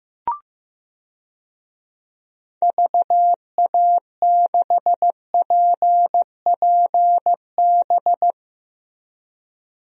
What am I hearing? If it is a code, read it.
Morse code, heard as a keyed tone, about 16 wpm, 700 Hz. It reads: VA6PPB
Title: DXCC-100 Call Signs - 6 Characters - Repeated 2x - 15wpm